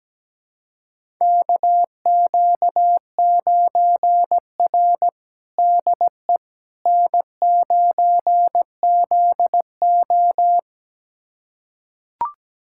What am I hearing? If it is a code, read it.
Morse code, heard as a keyed tone, about 17 wpm, 700 Hz. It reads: KQ9R DE N9ZO
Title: USA Callsign DE Callsign - 4 Characters - 17wpm